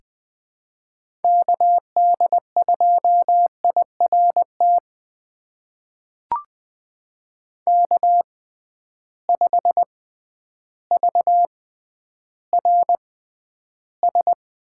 Morse code, heard as a keyed tone, about 20 wpm, spaced as for 11 wpm, 700 Hz. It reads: KD2IRT K5VRS
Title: USA Call Signs - 6x Character Spacing - 20wpm